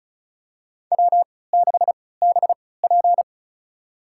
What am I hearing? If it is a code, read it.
Morse code, heard as a keyed tone, about 36 wpm, spaced as for 20 wpm, 700 Hz. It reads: W6BP
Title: USA Call Signs - 3x Character Spacing - 35wpm